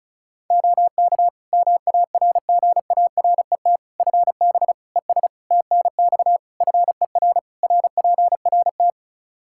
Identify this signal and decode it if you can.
Morse code, heard as a keyed tone, about 36 wpm, 700 Hz. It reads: OK MARGARET FB ES TNX FER RPRT
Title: Practice QSO - Exchange 2 - Info - 35wpm